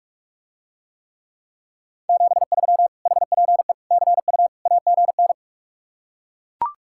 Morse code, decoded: 73 HPE CU AGN

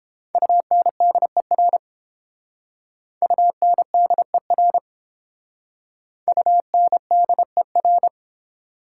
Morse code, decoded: UNDER UNDER UNDER

